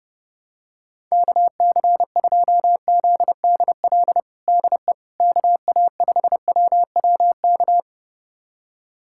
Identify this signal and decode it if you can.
Morse code, heard as a keyed tone, about 29 wpm, 700 Hz. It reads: KC2ZDL DE KA5WWK
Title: USA Callsign DE Callsign 30wpm